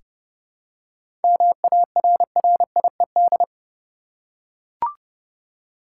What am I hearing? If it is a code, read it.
Morse code, heard as a keyed tone, about 31 wpm, 700 Hz. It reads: MARRIED
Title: Seven Letter Words - 30wpm